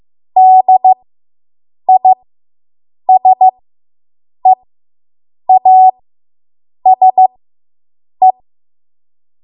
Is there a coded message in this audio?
Yes: Morse code from 0.4 to 8.3 s, heard as a keyed tone, about 15 wpm, spaced as for 7 wpm, 750 Hz. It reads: DISEASE